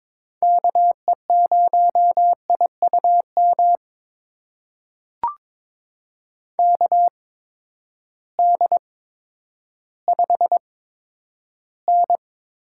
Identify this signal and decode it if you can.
Morse code, heard as a keyed tone, about 22 wpm, spaced as for 11 wpm, 700 Hz. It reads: KE0IUM KD5N
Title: USA Call Signs - 6 Characters - 8x Character Spacing - 22wpm